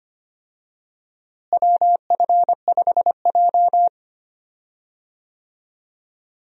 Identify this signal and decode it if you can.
Morse code, heard as a keyed tone, about 25 wpm, 700 Hz. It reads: WF5J
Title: USA Call Signs - 4 Characters - Repeated 2x - 25wpm